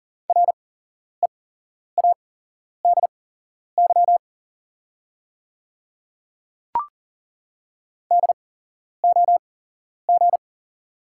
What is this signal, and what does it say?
Morse code, heard as a keyed tone, about 37 wpm, spaced as for 11 wpm, 690 Hz. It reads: READY DOG
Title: 8x Character Spacing - Top 500 Words - 40wpm